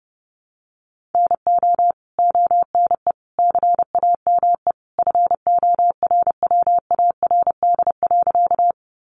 Morse code, heard as a keyed tone, about 30 wpm, 700 Hz. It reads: NO ONE CAME FORWARD.
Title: Sentences from Top 1300 Words 30wpm